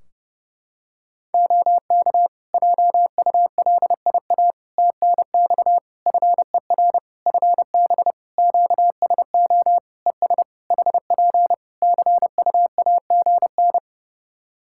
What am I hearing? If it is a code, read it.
Morse code, heard as a keyed tone, about 30 wpm, 700 Hz. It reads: OK JULIA TNX FER FB QSO ES HP CUAGN
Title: Practice QSO - Exchange 4 - Ending - 30wpm